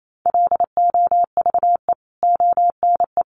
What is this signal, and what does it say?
Morse code, heard as a keyed tone, about 28 wpm, 700 Hz. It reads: LOVE ONE
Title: Sentences from Top 600 Words 28wpm